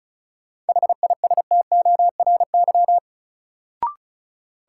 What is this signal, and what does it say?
Morse code, heard as a keyed tone, about 36 wpm, 700 Hz. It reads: HISTORY
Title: Sentences from Top 600 Words - New Word Drill - 35wpm